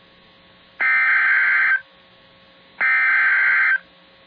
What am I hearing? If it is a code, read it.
Emergency Alert System (SAME) tones (header bursts), heard as harsh buzzing bursts. It is heard from 0.8 to 3.7 s.